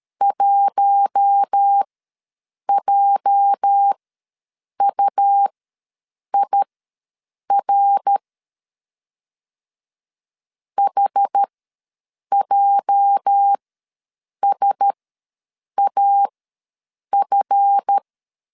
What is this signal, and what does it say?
Morse code, heard as a keyed tone, about 13 wpm, spaced as for 7 wpm, 800 Hz. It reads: 1JUIR HJSAF
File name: LEZIONI / 8 YJ45 / 8 YJ ws 12wpm 3342